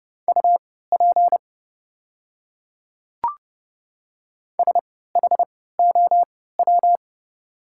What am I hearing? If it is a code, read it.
Morse code, heard as a keyed tone, about 31 wpm, spaced as for 17 wpm, 700 Hz. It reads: UP SHOW
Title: 3x Character Spacing - Top 200 Words - 30wpm